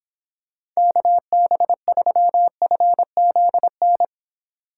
Morse code, heard as a keyed tone, about 23 wpm, 700 Hz. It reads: KB3FZN